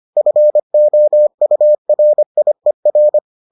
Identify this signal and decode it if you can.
Morse code, heard as a keyed tone, about 25 wpm, 600 Hz. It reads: FOURIER